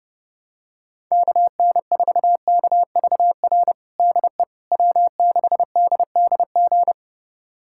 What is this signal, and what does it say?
Morse code, heard as a keyed tone, about 30 wpm, 700 Hz. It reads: KN4KVR DE W6DDG
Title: USA Callsign DE Callsign 30wpm